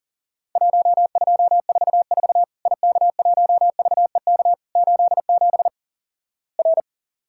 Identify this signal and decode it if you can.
Morse code, heard as a keyed tone, about 38 wpm, 700 Hz. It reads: 1244 IK1VEK 87 R
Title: Worked All Europe DX Contest - QTC Exchange - 40wpm